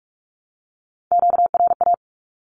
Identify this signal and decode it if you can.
Morse code, heard as a keyed tone, about 44 wpm, 700 Hz. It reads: QRA
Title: Q-Codes Repeated 3x 45wpm